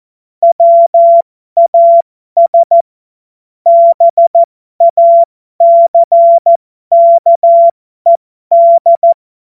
Morse code, heard as a keyed tone, about 14 wpm, spaced as for 12 wpm, 680 Hz. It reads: WAS BACKED
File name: Morsekurs / Witze